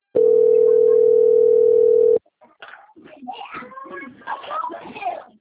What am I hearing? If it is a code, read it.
Ringback tone (North America), 440 + 480 Hz, with speech.
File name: out- - - - - .4235